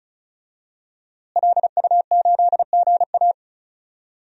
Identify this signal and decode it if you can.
Morse code, heard as a keyed tone, about 36 wpm, 700 Hz. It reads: LU8GA
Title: DXCC-100 Call Signs - 5 Characters - Repeated 2x - 35wpm